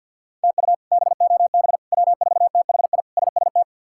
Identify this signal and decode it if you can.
Morse code, heard as a keyed tone, about 52 wpm, 700 Hz. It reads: TU BOB W4THI SST
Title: K1USN SST QSO Practice - 50wpm